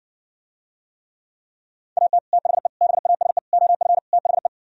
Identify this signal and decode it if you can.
Morse code, heard as a keyed tone, about 60 wpm, 700 Hz. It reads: AT THE BASE OF THE